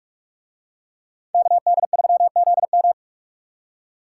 Morse code, heard as a keyed tone, about 44 wpm, 700 Hz. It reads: KD3ZM